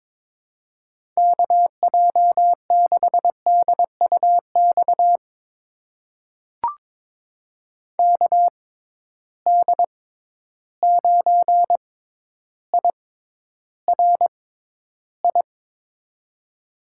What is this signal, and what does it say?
Morse code, heard as a keyed tone, about 22 wpm, spaced as for 11 wpm, 700 Hz. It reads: KJ6DUX KD9IRI
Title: USA Call Signs - 6 Characters - 6x Character Spacing - 22wpm